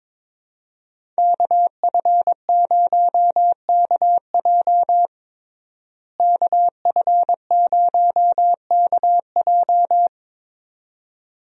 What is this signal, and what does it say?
Morse code, heard as a keyed tone, about 22 wpm, 700 Hz. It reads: KF0KJ KF0KJ
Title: USA Call Signs - 5 Characters - Repeated 2x - 22wpm